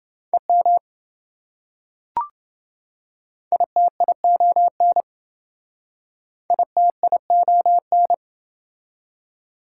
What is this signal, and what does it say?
Morse code, heard as a keyed tone, about 28 wpm, 700 Hz. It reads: EM ITION ITION